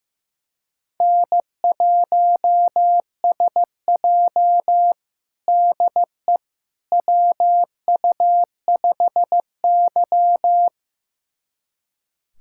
Morse code, heard as a keyed tone, about 15 wpm, 700 Hz. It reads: N1SJ DE WU5Y
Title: USA Callsign DE Callsign - 4 Characters - 15wpm